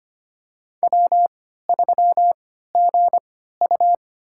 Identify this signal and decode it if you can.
Morse code, heard as a keyed tone, about 25 wpm, spaced as for 14 wpm, 700 Hz. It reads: W3GU
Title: USA Call Signs - 4 Characters - 3x Character Spacing - 25wpm